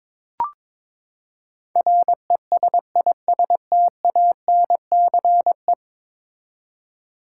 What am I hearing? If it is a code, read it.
Morse code, heard as a keyed tone, about 22 wpm, 700 Hz. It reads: RESISTANCE
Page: morse-code-ninja-practice-22wpm